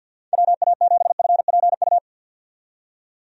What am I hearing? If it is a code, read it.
Morse code, heard as a keyed tone, about 50 wpm, 700 Hz. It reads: WA7FPU
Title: USA Call Signs - 6 Characters - 50wpm